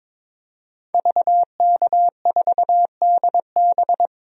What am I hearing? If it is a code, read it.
Morse code, heard as a keyed tone, about 21 wpm, 700 Hz. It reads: VK4DB
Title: DXCC-100 Call Signs - 5 Characters - Repeated 3x - 22wpm